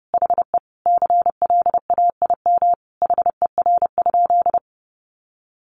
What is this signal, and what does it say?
Morse code, heard as a keyed tone, about 28 wpm, 700 Hz. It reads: HE CLAIM HER?
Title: Sentences from Top 1500 Words 30wpm